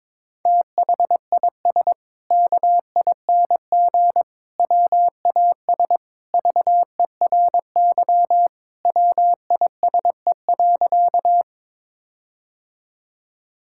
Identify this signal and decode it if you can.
Morse code, heard as a keyed tone, about 22 wpm, 700 Hz. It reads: THIS KING WAS VERY WISE.
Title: Sentences from Top 1600 Words 22wpm